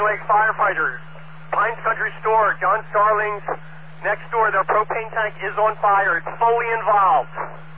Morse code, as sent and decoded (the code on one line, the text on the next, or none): none